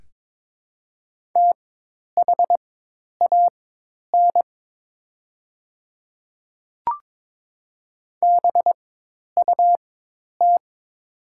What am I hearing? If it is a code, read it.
Morse code, heard as a keyed tone, about 22 wpm, spaced as for 10 wpm, 700 Hz. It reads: THAN BUT
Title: 4x Character Spacing - Top 100 Words - 22wpm